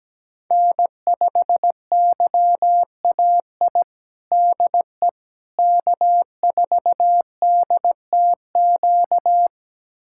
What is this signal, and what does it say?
Morse code, heard as a keyed tone, about 17 wpm, 700 Hz. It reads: N5YAI DE K4DTQ
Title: USA Callsign DE Callsign 17wpm